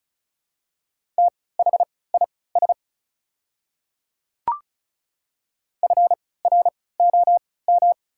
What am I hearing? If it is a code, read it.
Morse code, heard as a keyed tone, about 36 wpm, spaced as for 20 wpm, 690 Hz. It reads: THIS FROM
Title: 3x Character Spacing - Top 100 Words - 35wpm